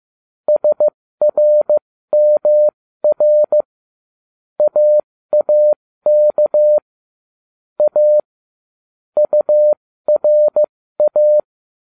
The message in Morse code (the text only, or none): SRMR AAK A URA